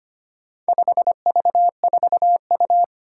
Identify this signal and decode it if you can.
Morse code, heard as a keyed tone, about 25 wpm, 700 Hz. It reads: 5V4U
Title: Four Challenging Letter-Number Combinations 25wpm